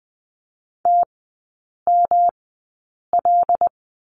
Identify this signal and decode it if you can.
Morse code, heard as a keyed tone, about 20 wpm, spaced as for 8 wpm, 700 Hz. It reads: TML
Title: 127 Lesson 21e - Number 6 - Three Characters 20wpm